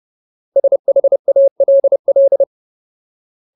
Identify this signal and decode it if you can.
Morse code, heard as a keyed tone, about 30 wpm, 550 Hz. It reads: SHALL
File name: Ultimate Word List / 30WPM Ultimate